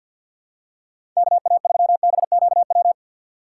Morse code, heard as a keyed tone, about 53 wpm, 700 Hz. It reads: KA3BQW
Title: USA Call Signs - 6 Characters - 3x Character Spacing - 50wpm